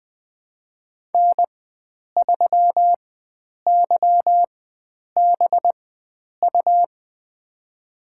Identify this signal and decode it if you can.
Morse code, heard as a keyed tone, about 20 wpm, spaced as for 9 wpm, 700 Hz. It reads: N3YBU
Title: USA Call Signs - 5 Characters - 4x Character Spacing - 20wpm